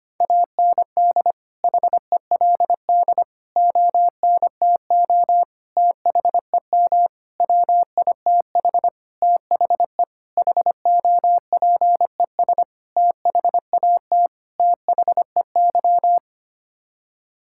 Morse code, heard as a keyed tone, about 25 wpm, 700 Hz. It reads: AND HELD ONTO THEM WITH THE HOPES THAT THEY
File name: Sets of 9 Words - 25wpm